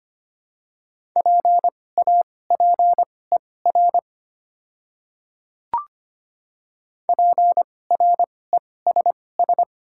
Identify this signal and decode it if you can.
Morse code, heard as a keyed tone, about 25 wpm, spaced as for 18 wpm, 700 Hz. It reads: PAPER PRESS
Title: 2x Character Spacing - Top 300 Words - 25wpm